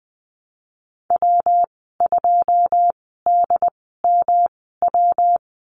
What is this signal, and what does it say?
Morse code, heard as a keyed tone, about 20 wpm, spaced as for 15 wpm, 700 Hz. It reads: W2DMW
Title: 200 Lesson 32g - Letter k - Call Signs 20wpm